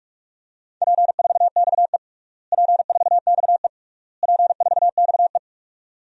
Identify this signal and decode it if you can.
Morse code, heard as a keyed tone, about 46 wpm, 700 Hz. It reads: P4XE P4XE P4XE
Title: DXCC-100 Call Signs - 4 Characters - Repeated 3x - 45wpm